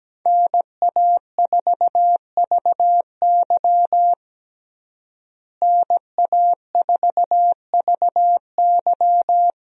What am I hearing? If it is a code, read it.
Morse code, heard as a keyed tone, about 17 wpm, 700 Hz. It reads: NA4VY NA4VY
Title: USA Call Signs - 5 Characters - Repeated 2x - 17wpm